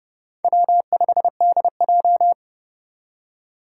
Morse code, heard as a keyed tone, about 30 wpm, 700 Hz. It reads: W5DJ
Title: Four Challenging Letter-Number Combinations 30wpm